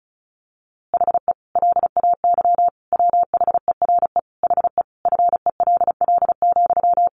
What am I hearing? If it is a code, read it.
Morse code, heard as a keyed tone, about 34 wpm, 700 Hz. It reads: HE LAY WHERE HE FELL,
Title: A Sound Of Thunder 35wpm